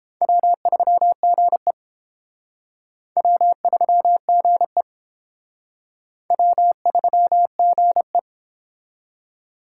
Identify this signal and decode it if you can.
Morse code, heard as a keyed tone, about 31 wpm, 700 Hz. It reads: W3GE W3GE W3GE